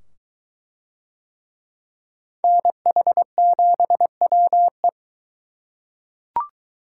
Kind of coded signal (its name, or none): Morse code